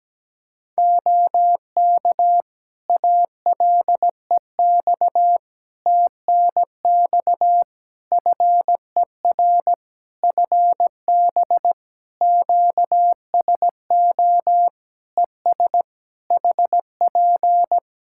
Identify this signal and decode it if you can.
Morse code, heard as a keyed tone, about 17 wpm, 700 Hz. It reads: OK ALEX TNX FER FB QSO ES HP
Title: Practice QSO - Exchange 4 - Ending - 17wpm